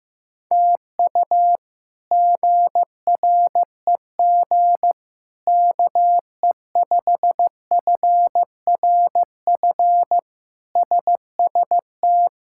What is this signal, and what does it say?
Morse code, heard as a keyed tone, about 15 wpm, 700 Hz. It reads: TU GREG KE5FRF SST
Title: K1USN SST QSO Practice - 15wpm